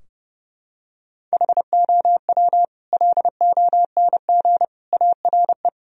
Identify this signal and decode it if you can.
Morse code, heard as a keyed tone, about 30 wpm, 700 Hz. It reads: HOW LONG ARE
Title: Sentences from Top 1000 Words - Review Part 2 - 30wpm